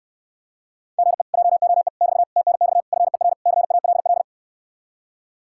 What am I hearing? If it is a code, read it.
Morse code, heard as a keyed tone, about 69 wpm, 700 Hz. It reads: GE JOE <BT> TNX FER CALL